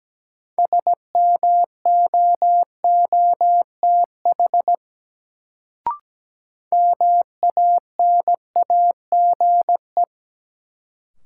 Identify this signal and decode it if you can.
Morse code, heard as a keyed tone, about 17 wpm, 700 Hz. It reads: SMOOTH MANAGE
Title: Sentences from Top 1400 Words - New Word Drill - 17wpm